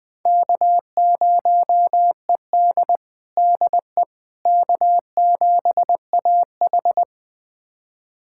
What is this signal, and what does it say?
Morse code, heard as a keyed tone, about 20 wpm, 700 Hz. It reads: K0ED DE K7AH
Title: USA Callsign DE Callsign - 4 Characters - 20wpm